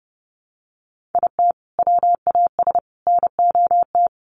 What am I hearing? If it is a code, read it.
Morse code, heard as a keyed tone, about 30 wpm, 700 Hz. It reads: IT WAS NOT
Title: Sets of 3 Words 30wpm